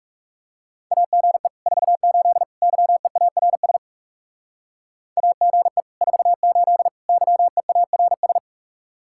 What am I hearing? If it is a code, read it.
Morse code, heard as a keyed tone, about 36 wpm, 700 Hz. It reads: AGE 48 YEARS AGE 48 YEARS
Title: Speed-Racing Common QSO Phrases - 30wpm